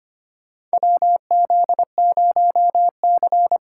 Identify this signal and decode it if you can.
Morse code, heard as a keyed tone, about 25 wpm, 700 Hz. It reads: WZ0C
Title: USA Call Signs - 2x Character Spacing - 25wpm